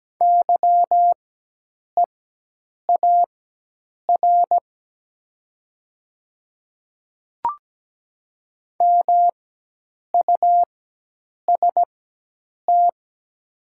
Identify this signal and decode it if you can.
Morse code, heard as a keyed tone, about 17 wpm, spaced as for 8 wpm, 690 Hz. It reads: YEAR MUST
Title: 4x Character Spacing - Top 200 Words - 17wpm